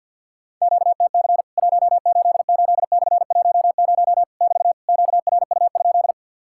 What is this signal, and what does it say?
Morse code, heard as a keyed tone, about 50 wpm, 700 Hz. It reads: QTC 188/10 <BT> QRU?
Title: Worked All Europe DX Contest - QTC Exchange - 50wpm